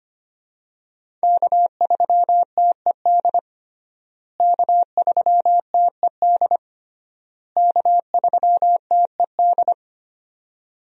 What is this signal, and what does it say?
Morse code, heard as a keyed tone, about 25 wpm, 700 Hz. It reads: K3TED K3TED K3TED
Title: USA Call Signs - 5 Characters - Repeated 3x - 25wpm